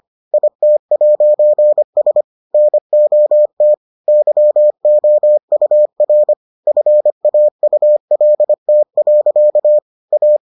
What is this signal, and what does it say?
Morse code, heard as a keyed tone, about 25 wpm, 600 Hz. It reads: IT'S NOT YOUR FAULT. A